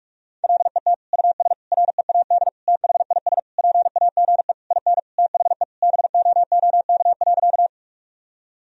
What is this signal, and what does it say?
Morse code, heard as a keyed tone, about 46 wpm, 700 Hz. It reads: LET US READ THIS PAGE IN THE BOOK.
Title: Sentences from Top 1000 Words - Review Part 1 - 45wpm